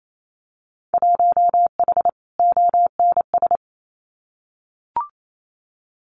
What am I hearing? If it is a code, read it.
Morse code, heard as a keyed tone, about 28 wpm, 700 Hz. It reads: 1H ONS
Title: Winter Field Day Exchanges 28wpm